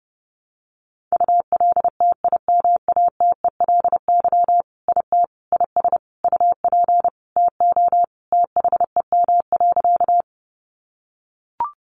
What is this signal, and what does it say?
Morse code, heard as a keyed tone, about 30 wpm, 700 Hz. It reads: ULTIMATELY IT IS UP TO THEM.
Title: Sentences from Top 1200 Words 30wpm